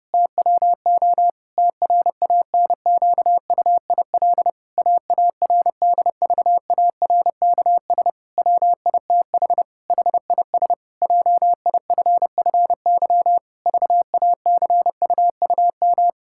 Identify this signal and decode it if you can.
Morse code, heard as a keyed tone, about 32 wpm, 700 Hz. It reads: TWO TRANQUIL AARDVARKS WITH HIS JIFFY VACUUM
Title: Pangram Challenge v2 - 30wpm